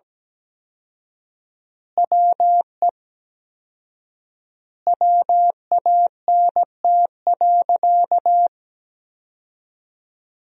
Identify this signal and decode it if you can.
Morse code, heard as a keyed tone, about 17 wpm, 700 Hz. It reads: WE WANT.
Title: 4x Word-Spacing - Sentences from Top 200 Words - 17wpm